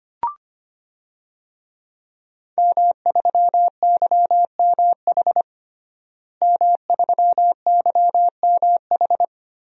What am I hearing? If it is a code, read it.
Morse code, heard as a keyed tone, about 25 wpm, 700 Hz. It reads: M3YMH M3YMH
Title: DXCC-100 Call Signs - 5 Characters - Repeated 2x - 25wpm